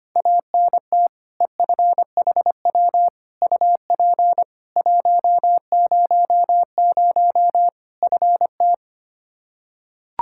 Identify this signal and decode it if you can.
Morse code, heard as a keyed tone, about 25 wpm, 700 Hz. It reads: ANT EFHW UP 100 FT
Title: Practice QSO - Exchange 2 - Info - 25wpm